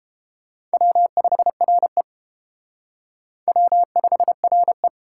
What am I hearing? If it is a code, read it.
Morse code, heard as a keyed tone, about 29 wpm, 700 Hz. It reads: W5RE W5RE